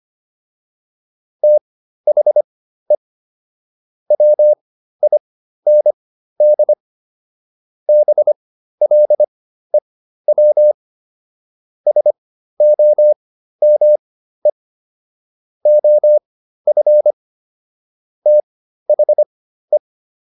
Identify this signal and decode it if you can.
Morse code, heard as a keyed tone, about 25 wpm, spaced as for 13 wpm, 600 Hz. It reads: THE WIND BLEW SOME OF THE